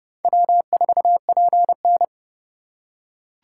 Morse code, heard as a keyed tone, about 29 wpm, 700 Hz. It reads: W4PN